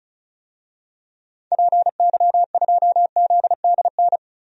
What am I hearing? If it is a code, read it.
Morse code, heard as a keyed tone, about 36 wpm, 700 Hz. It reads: PY2ZDN
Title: DXCC-100 Call Signs - 6 Characters - Repeated 3x - 35wpm